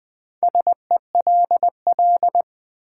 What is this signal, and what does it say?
Morse code, heard as a keyed tone, about 20 wpm, 700 Hz. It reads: SELL
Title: Sentences from Top 1000 Words - Review Part 6 - New Word Drill - 20wpm